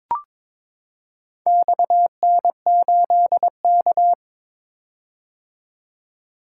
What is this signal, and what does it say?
Morse code, heard as a keyed tone, about 22 wpm, 700 Hz. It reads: XN8K
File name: practice / 22wpm / Four Letter-Number Combinations - 22wpm